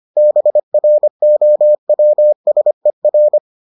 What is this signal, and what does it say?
Morse code, heard as a keyed tone, about 25 wpm, 600 Hz. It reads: BROWSER